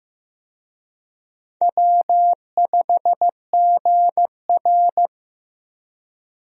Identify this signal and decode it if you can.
Morse code, heard as a keyed tone, about 15 wpm, 700 Hz. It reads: W5GR
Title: USA Call Signs - 4 Characters - Repeated 3x - 15wpm